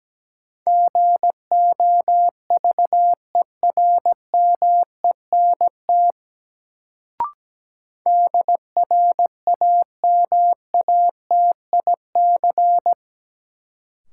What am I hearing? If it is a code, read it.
Morse code, heard as a keyed tone, about 17 wpm, 700 Hz. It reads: GOVERMENT DRAMATIC
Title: Sentences from Top 1300 Words - New Word Drill - 17wpm